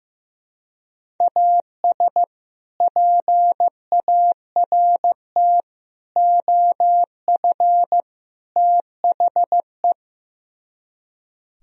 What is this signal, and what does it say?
Morse code, heard as a keyed tone, about 17 wpm, 700 Hz. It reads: AS PART OF THE
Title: Speed-Racing - Sets of 4 Words 15wpm